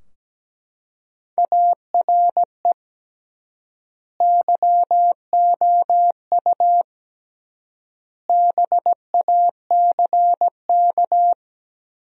Morse code, heard as a keyed tone, about 17 wpm, 700 Hz. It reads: ARE YOU BACK
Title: 3x Word-Spacing - Sentences from Top 200 Words - 17wpm